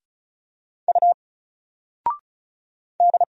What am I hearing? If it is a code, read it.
Morse code, heard as a keyed tone, about 36 wpm, 700 Hz. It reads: U D
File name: Single-Letter-Number-35wpm